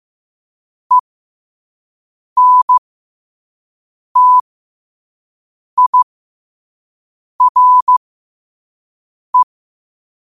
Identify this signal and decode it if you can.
Morse code, heard as a keyed tone, about 15 wpm, spaced as for 5 wpm, 1000 Hz. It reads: ENTIRE
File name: qotdfiles / 5wpm